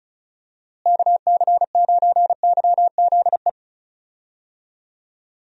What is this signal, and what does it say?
Morse code, heard as a keyed tone, about 36 wpm, 700 Hz. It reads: KC9YZE
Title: USA Call Signs - 6 Characters - Repeated 3x - 35wpm